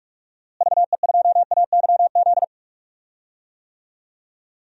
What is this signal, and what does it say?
Morse code, heard as a keyed tone, about 46 wpm, 700 Hz. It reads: VE2AYZ
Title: DXCC-100 Call Signs - 6 Characters - Repeated 3x - 45wpm